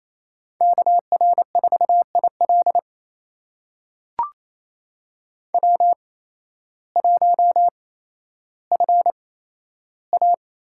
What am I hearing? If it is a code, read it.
Morse code, heard as a keyed tone, about 28 wpm, spaced as for 13 wpm, 700 Hz. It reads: KR4IL W1FA
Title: USA Call Signs - 8x Character Spacing - 28wpm